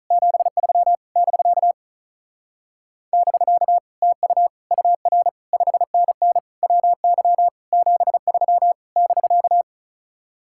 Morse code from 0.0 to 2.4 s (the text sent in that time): 73 <BK>